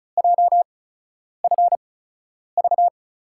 Morse code, heard as a keyed tone, about 34 wpm, spaced as for 10 wpm, 700 Hz. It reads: JFV
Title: USA Call Signs - 5 Characters - 8x Character Spacing - 35wpm